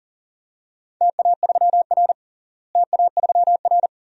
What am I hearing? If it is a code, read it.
Morse code, heard as a keyed tone, about 40 wpm, 690 Hz. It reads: TA3R TA3R
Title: DXCC-100 Call Signs - 4 Characters - Repeated 2x - 40wpm